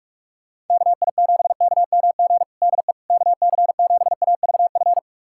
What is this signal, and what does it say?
Morse code, heard as a keyed tone, about 46 wpm, 700 Hz. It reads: KI7KMG DE KC7AVF
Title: USA Callsign DE Callsign - 6 Characters - 45wpm